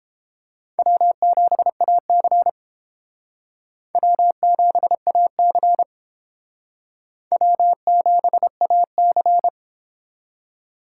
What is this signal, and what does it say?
Morse code, heard as a keyed tone, about 30 wpm, 700 Hz. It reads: W7AC W7AC W7AC